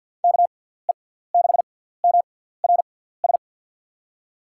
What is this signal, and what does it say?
Morse code, heard as a keyed tone, about 47 wpm, 700 Hz. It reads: K E 6 M R S
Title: USA Call Signs - 6 Characters - 6x Character Spacing - 50wpm